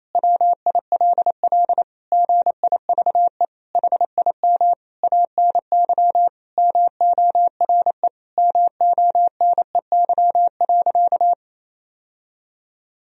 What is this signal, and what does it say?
Morse code, heard as a keyed tone, about 28 wpm, 700 Hz. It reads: WILL GIVE HIM ANY MORE MONEY.
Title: Sentences from Top 1000 Words - Review Part 4 - 28wpm